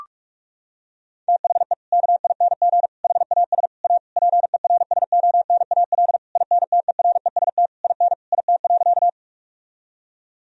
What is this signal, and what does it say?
Morse code, heard as a keyed tone, about 46 wpm, 700 Hz. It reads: THE KING HAS A PERSONAL INTEREST IN IT.
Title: Sentences from Top 1000 Words - Review Part 4 - 45wpm